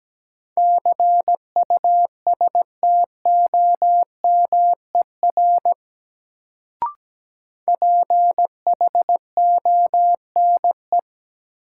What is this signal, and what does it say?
Morse code, heard as a keyed tone, about 17 wpm, 700 Hz. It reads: CUSTOMER PHONE
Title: Sentences from Top 1100 Words - New Word Drill - 17wpm